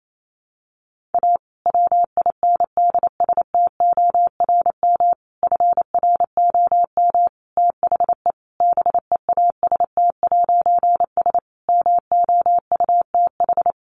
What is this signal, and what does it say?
Morse code, heard as a keyed tone, about 28 wpm, 700 Hz. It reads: A WINDSTORM FROM THE BEAST'S MOUTH
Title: A Sound Of Thunder 28wpm